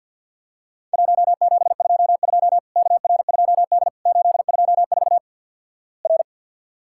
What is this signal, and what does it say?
Morse code, heard as a keyed tone, about 48 wpm, 700 Hz. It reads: 1722 KR2D 824 R
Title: Worked All Europe DX Contest - QTC Exchange - 50wpm